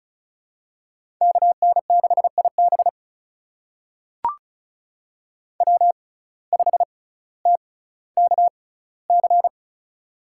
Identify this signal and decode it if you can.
Morse code, heard as a keyed tone, about 36 wpm, spaced as for 18 wpm, 700 Hz. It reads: KN6IB W5TKC
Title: USA Call Signs - 5 Characters - 6x Character Spacing - 35wpm